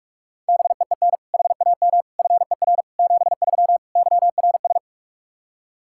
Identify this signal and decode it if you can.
Morse code, heard as a keyed tone, about 46 wpm, 700 Hz. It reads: BEEN HAM FER 73 YRS
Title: Practice QSO - Exchange 3 - More Info - 45wpm